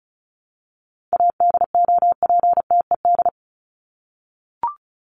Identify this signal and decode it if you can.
Morse code, heard as a keyed tone, about 34 wpm, 700 Hz. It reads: ADOPTED
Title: ED Suffix - Top 100 Words with Base Words 35wpm